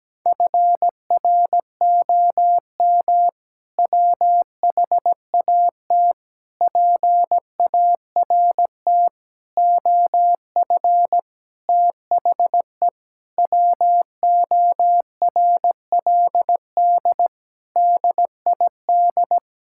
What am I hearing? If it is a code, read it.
Morse code, heard as a keyed tone, about 17 wpm, 700 Hz. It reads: FROM WHAT PART OF THE WORLD DID
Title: Sentences from Top 1000 Words - Review Part 6 - 17wpm